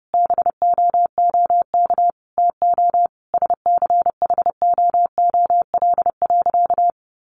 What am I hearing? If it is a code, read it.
Morse code, heard as a keyed tone, about 30 wpm, 700 Hz. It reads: BOOK TO SCHOOL.
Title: Sentences from Top 1400 Words 30wpm